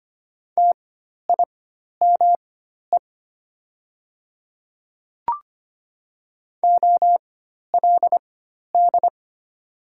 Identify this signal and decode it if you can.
Morse code, heard as a keyed tone, about 25 wpm, spaced as for 12 wpm, 700 Hz. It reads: TIME OLD
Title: 4x Character Spacing - Top 200 Words - 25wpm